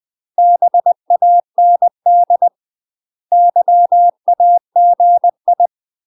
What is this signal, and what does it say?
Morse code, heard as a keyed tone, about 20 wpm, 700 Hz. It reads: BAND YAGI